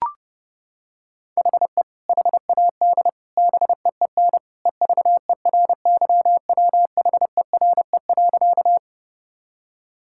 Morse code, heard as a keyed tone, about 30 wpm, 700 Hz. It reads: HE HAD BEEN EVERYWHERE.
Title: Sentences from Top 1600 Words 30wpm